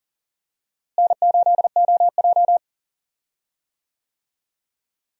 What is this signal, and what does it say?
Morse code, heard as a keyed tone, about 40 wpm, 700 Hz. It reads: N8OJ
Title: USA Call Signs - 4 Characters - Repeated 2x - 40wpm